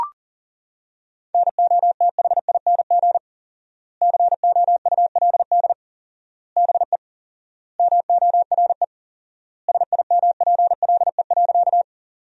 Morse code, heard as a keyed tone, about 40 wpm, 700 Hz. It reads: NOTHING COULD BE MORE SIMPLE.
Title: 4x Word-Spacing - Sentences from Top 500 Words - 40wpm